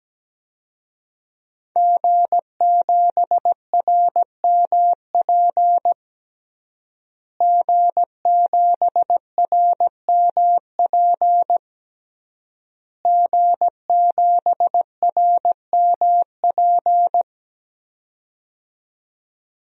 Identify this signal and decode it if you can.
Morse code, heard as a keyed tone, about 17 wpm, 700 Hz. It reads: G7RMP G7RMP G7RMP
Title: DXCC-100 Call Signs - 5 Characters - Repeated 3x - 17wpm